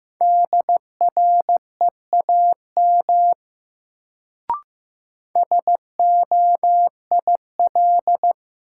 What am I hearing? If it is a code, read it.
Morse code, heard as a keyed tone, about 16 wpm, 700 Hz. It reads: DREAM SOIL
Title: Sentences from Top 1500 Words - New Word Drill Alt - 15wpm